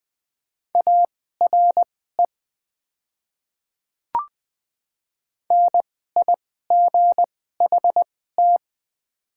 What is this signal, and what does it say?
Morse code, heard as a keyed tone, about 21 wpm, spaced as for 15 wpm, 700 Hz. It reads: ARE NIGHT
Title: 2x Character Spacing - Top 500 Words - 20wpm